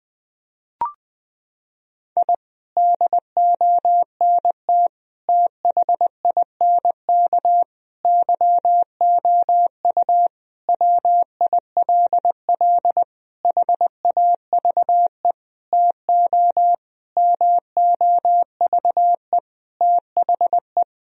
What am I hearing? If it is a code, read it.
Morse code, heard as a keyed tone, about 21 wpm, 700 Hz. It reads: I DONT THINK YOU WILL HAVE TO MOVE THE
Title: Sentences from Top 1000 Words - Review Part 4 - 20wpm